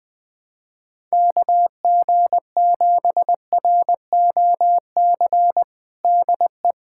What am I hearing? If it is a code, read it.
Morse code, heard as a keyed tone, about 20 wpm, 700 Hz. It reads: KG7ROC DE
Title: USA Callsign DE Callsign - 6 Characters - 20wpm